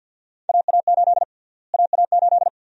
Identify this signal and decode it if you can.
Morse code, heard as a keyed tone, about 52 wpm, 700 Hz. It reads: AA8 AA8
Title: USA Call Sign Prefixes - Repeated 2x - 50wpm